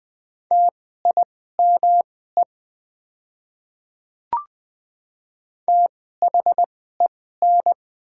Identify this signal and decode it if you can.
Morse code, heard as a keyed tone, about 20 wpm, spaced as for 14 wpm, 700 Hz. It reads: TIME THEN
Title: 2x Character Spacing - Top 300 Words - 20wpm